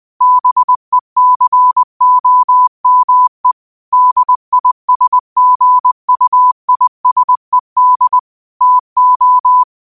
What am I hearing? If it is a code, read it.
Morse code, heard as a keyed tone, about 20 wpm, 1000 Hz. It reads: BECOME DISGUISED TO
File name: qotdfiles / 20wpm